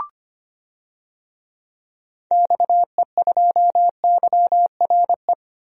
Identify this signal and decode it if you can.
Morse code, heard as a keyed tone, about 25 wpm, 700 Hz. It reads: XE2YRE
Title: DXCC-100 Call Signs - 6 Characters - Repeated 2x - 25wpm